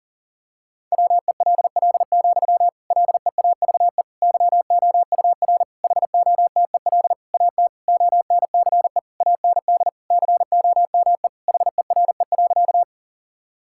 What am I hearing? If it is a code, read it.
Morse code, heard as a keyed tone, about 40 wpm, 700 Hz. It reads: WELL, LEAVE YOUR HOTEL AT ONCE AND COME HERE.
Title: Sentences from Top 1600 Words 40wpm